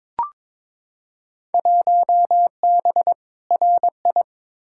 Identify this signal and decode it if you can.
Morse code, heard as a keyed tone, about 21 wpm, 700 Hz. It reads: 1B RI
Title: Field Day Exchanges Encore - 22wpm